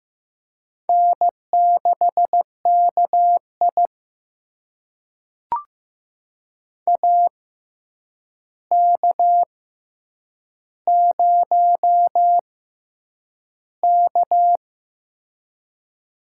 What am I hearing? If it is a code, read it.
Morse code, heard as a keyed tone, about 15 wpm, spaced as for 8 wpm, 700 Hz. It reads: N6KI AK0K
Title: USA Call Signs - 4 Characters - 6x Character Spacing - 15wpm